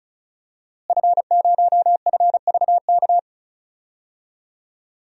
Morse code, heard as a keyed tone, about 36 wpm, 700 Hz. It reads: F0FVK
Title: DXCC-100 Call Signs - 5 Characters - Repeated 3x - 35wpm